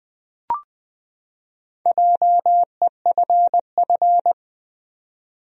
Morse code, heard as a keyed tone, about 20 wpm, 700 Hz. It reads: JEFF
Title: USA Common Names - Max 4 Letters - 20wpm